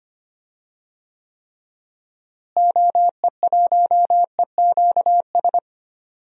Morse code, heard as a keyed tone, about 25 wpm, 700 Hz. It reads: OE1EQS
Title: DXCC-100 Call Signs - 6 Characters - Repeated 3x - 25wpm